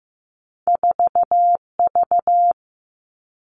Morse code, heard as a keyed tone, about 15 wpm, 700 Hz. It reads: 4 V
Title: Two Challenging Letter-Number Combinations 15wpm